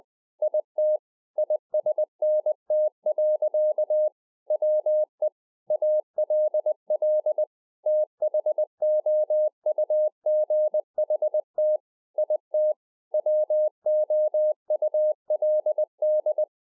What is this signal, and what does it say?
Morse code, heard as a keyed tone, about 20 wpm, 610 Hz. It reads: IT ISNT. WE ALL THOUGHT IT WOULD